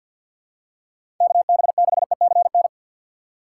Morse code, heard as a keyed tone, about 53 wpm, 700 Hz. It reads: KB6ECN
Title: USA Call Signs - 6 Characters - 8x Character Spacing - 50wpm